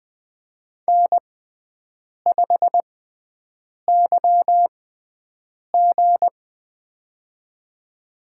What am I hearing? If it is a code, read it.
Morse code, heard as a keyed tone, about 20 wpm, spaced as for 7 wpm, 700 Hz. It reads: N5YG
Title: USA Call Signs - 6x Character Spacing - 20wpm